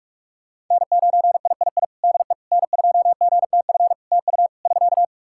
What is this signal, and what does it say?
Morse code, heard as a keyed tone, about 46 wpm, 700 Hz. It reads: N9III DE N2GTF TU <SK>
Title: Practice QSO - Exchange 4 - Ending - 45wpm